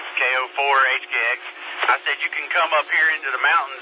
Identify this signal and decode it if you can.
Emergency Alert System (SAME) tones, heard as harsh buzzing bursts, with speech. It is heard from 0.8 to 3.6 s.